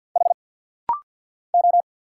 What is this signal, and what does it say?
Morse code, heard as a keyed tone, about 48 wpm, 700 Hz. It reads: H O
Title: Single Letter-Number - Rapid-Fire 50wpm